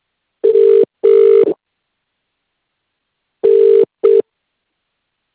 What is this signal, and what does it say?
Ringback tone (UK), 400 + 450 Hz.